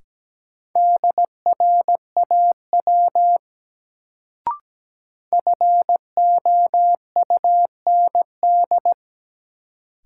Morse code, heard as a keyed tone, about 17 wpm, 700 Hz. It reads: DRAW FOUND
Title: Sentences from Top 300 Words - New Word Drill - 17wpm